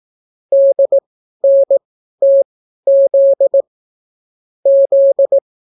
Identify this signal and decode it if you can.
Morse code, heard as a keyed tone, about 18 wpm, spaced as for 12 wpm, 550 Hz. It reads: DNTZ Z